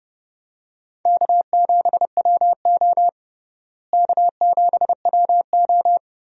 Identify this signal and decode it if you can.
Morse code, heard as a keyed tone, about 30 wpm, 700 Hz. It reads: K7WO K7WO
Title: USA Call Signs - 4 Characters - Repeated 2x - 30wpm